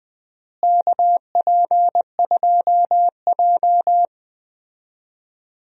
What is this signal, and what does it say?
Morse code, heard as a keyed tone, about 20 wpm, 700 Hz. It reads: KP2J
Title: DXCC-100 Call Signs - 4 Characters - Repeated 2x - 20wpm